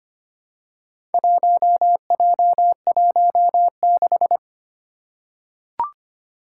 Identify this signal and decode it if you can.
Morse code, heard as a keyed tone, about 25 wpm, 700 Hz. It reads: 1J16
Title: Four Challenging Letter-Number Combinations 25wpm